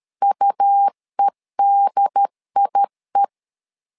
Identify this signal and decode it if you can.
Morse code, heard as a keyed tone, about 13 wpm, 800 Hz. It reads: UEDIE
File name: LEZIONI / 3 MWRO / 3 MWR 12wpm 1711